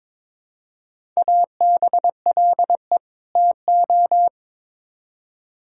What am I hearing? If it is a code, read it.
Morse code, heard as a keyed tone, about 22 wpm, 700 Hz. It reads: ABLE TO
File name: Build A Phrase - 2-Word Phrases - 22wpm